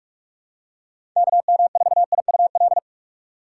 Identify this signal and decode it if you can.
Morse code, heard as a keyed tone, about 47 wpm, 700 Hz. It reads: KM4IUL
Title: USA Call Signs - 6 Characters - 3x Character Spacing - 45wpm